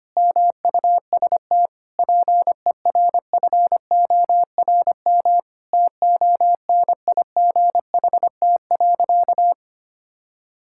Morse code, heard as a keyed tone, about 25 wpm, 700 Hz. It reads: MUST PERFORM TONIGHT.
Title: Sentences from Top 1000 Words - Review Part 5 - 25wpm